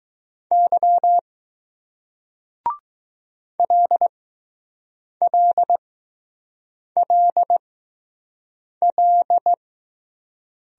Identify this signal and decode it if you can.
Morse code, heard as a keyed tone, about 20 wpm, spaced as for 6 wpm, 700 Hz. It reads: Y LLLL